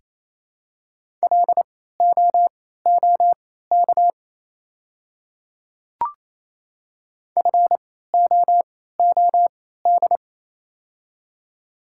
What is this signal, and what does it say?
Morse code, heard as a keyed tone, about 28 wpm, spaced as for 16 wpm, 700 Hz. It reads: LOOK FOOD
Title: 3x Character Spacing - Top 300 Words - 28wpm